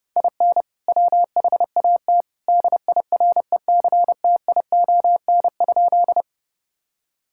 Morse code, heard as a keyed tone, about 30 wpm, 700 Hz. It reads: IN WHAT DIRECTION?
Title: Sentences from Top 1000 Words - Review Part 5 - 30wpm